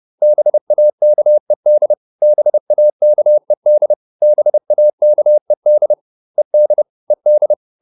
Morse code, heard as a keyed tone, about 30 wpm, 600 Hz. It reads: BAKED BAKED BAKED ED ED